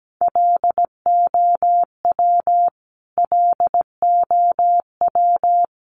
Morse code, heard as a keyed tone, about 17 wpm, 700 Hz. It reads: LOW LOW
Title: Top 300 Words Repeated 17wpm